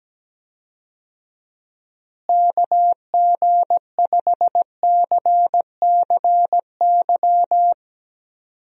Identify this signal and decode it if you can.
Morse code, heard as a keyed tone, about 17 wpm, 700 Hz. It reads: KG5CCY
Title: USA Call Signs - 6 Characters - Repeated 2x - 17wpm